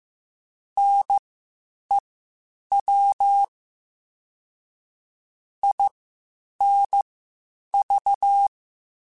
Morse code, heard as a keyed tone, about 15 wpm, spaced as for 8 wpm, 800 Hz. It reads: NEW INV
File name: QSOs / QSOs-mp3s / WD0FBT 1@13